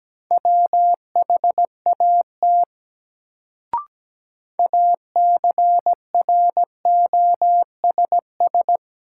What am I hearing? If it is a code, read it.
Morse code, heard as a keyed tone, about 17 wpm, 700 Hz. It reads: WHAT ACROSS
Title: Sets of 2 Words - New Word Drill - 17wpm